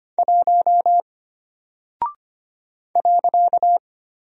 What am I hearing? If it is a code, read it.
Morse code, heard as a keyed tone, about 25 wpm, 700 Hz. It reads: 1 .
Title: Single Character (Letter:Number:Punctuation:Prosign) Alt 25wpm